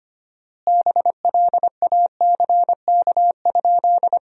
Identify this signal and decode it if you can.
Morse code, heard as a keyed tone, about 25 wpm, 700 Hz. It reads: BLACK?
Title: 4x Word-Spacing - Sentences from Top 400 Words - 25wpm